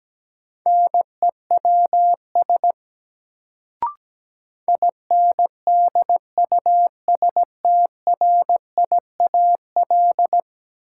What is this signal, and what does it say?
Morse code, heard as a keyed tone, about 17 wpm, 700 Hz. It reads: NEWS INDUSTRIAL
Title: Sets of 5 Words - New Word Drill - 17wpm